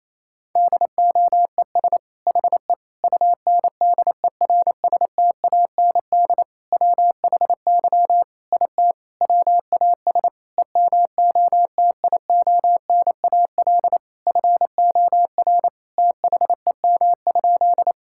Morse code, heard as a keyed tone, about 27 wpm, 700 Hz. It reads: DOES HE UNDERSTAND WHY IT WAS EMOTIONAL FOR THEM?
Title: Sentences from Top 1000 Words - Review Part 5 - 28wpm